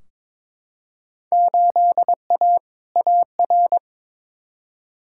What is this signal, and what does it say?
Morse code, heard as a keyed tone, about 22 wpm, 700 Hz. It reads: 8A AR
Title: Field Day Exchanges Encore - 22wpm